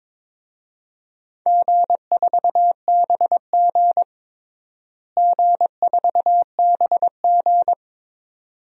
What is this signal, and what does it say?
Morse code, heard as a keyed tone, about 22 wpm, 700 Hz. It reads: G4BG G4BG
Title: DXCC-100 Call Signs - 4 Characters - Repeated 2x - 22wpm